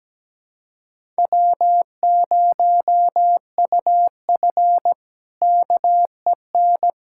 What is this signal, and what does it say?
Morse code, heard as a keyed tone, about 17 wpm, 700 Hz. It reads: W0UF KEN
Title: K1USN SST QSO Practice - 17wpm